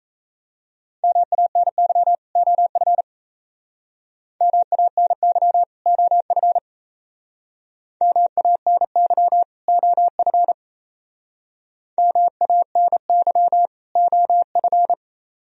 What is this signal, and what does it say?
Morse code, heard as a keyed tone, about 36 wpm, 700 Hz. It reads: MANY OF MANY OF MANY OF MANY OF